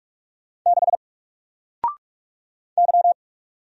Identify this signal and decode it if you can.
Morse code, heard as a keyed tone, about 46 wpm, 700 Hz. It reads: 6 Y
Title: Single Letter-Number 45wpm